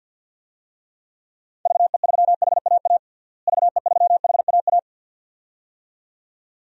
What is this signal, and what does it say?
Morse code, heard as a keyed tone, about 52 wpm, 700 Hz. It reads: VE3HAA VE3HAA
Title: DXCC-100 Call Signs - 6 Characters - Repeated 2x - 50wpm